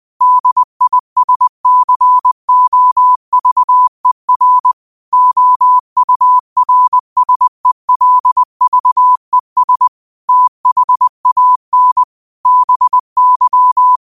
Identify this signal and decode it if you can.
Morse code, heard as a keyed tone, about 20 wpm, 1000 Hz. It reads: DISCOVER OURSELVES THAN BY
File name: qotdfiles / 20wpm